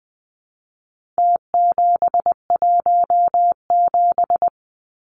Morse code, heard as a keyed tone, about 20 wpm, 700 Hz. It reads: T717
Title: Numbers - Contest Style - 4-Digit 20wpm